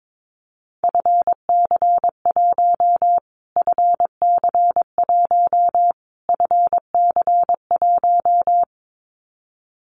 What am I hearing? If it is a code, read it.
Morse code, heard as a keyed tone, about 22 wpm, 700 Hz. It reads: FC1 FC1 FC1
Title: Callsign Prefixes for Top 25 DXCC - Repeated 3x - 22wpm